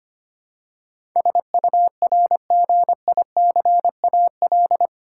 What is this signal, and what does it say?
Morse code, heard as a keyed tone, about 25 wpm, 700 Hz. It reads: SURGICAL
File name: Eight Letter Words - 25wpm